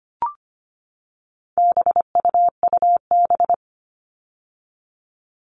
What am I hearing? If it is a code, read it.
Morse code, heard as a keyed tone, about 25 wpm, 700 Hz. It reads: BUUB
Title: All Cut Numbers - 4-Digit 25wpm